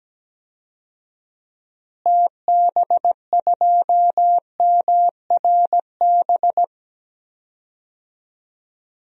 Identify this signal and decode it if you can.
Morse code, heard as a keyed tone, about 17 wpm, 700 Hz. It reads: TB2MRB
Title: DXCC-100 Call Signs - 6 Characters - Repeated 2x - 17wpm